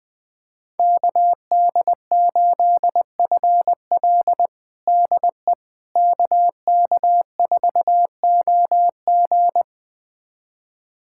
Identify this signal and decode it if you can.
Morse code, heard as a keyed tone, about 21 wpm, 700 Hz. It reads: KD8FL DE KK4OG
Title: USA Callsign DE Callsign - 5 Characters - 20wpm